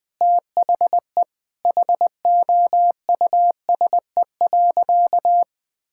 Morse code, heard as a keyed tone, about 20 wpm, 700 Hz. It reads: THE HOUSE.
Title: Sentences from Top 1600 Words 20wpm